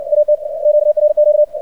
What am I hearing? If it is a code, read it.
Morse code, heard as a keyed tone, about 59 wpm, 600 Hz. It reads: AT QRQ
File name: archived / CRSnet